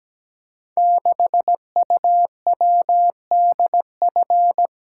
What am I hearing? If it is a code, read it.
Morse code, heard as a keyed tone, about 17 wpm, 700 Hz. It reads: 6UWDF
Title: Code Groups - 5 Letter:Numbers - 17wpm